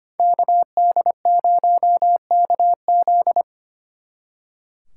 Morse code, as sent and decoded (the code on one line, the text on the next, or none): -.- -.. ----- -.- --..
KD0KZ